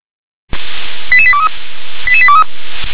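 Five-tone selective call (ZVEI), heard as quick runs of five tones, tones of about 70 ms. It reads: 88823 88823
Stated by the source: ZVEI five-tone call 88914 received on an SDR.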